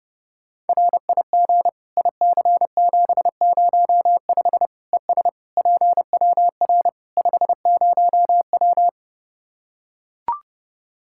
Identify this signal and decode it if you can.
Morse code, heard as a keyed tone, about 30 wpm, 700 Hz. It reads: RIG IC705 ES PWR 50W
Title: K1USN SST QSO Practice 30wpm